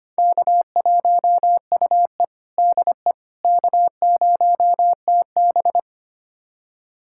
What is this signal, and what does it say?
Morse code, heard as a keyed tone, about 25 wpm, 700 Hz. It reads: K1UE DE K0TB
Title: USA Callsign DE Callsign - 4 Characters - 25wpm